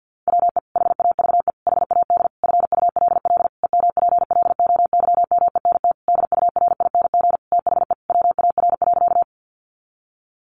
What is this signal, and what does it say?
Morse code, heard as a keyed tone, about 48 wpm, 700 Hz. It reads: WE HAVE HAD FULL EMPLOYMENT DURING THE WAR.
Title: Sentences from Top 1500 Words 50wpm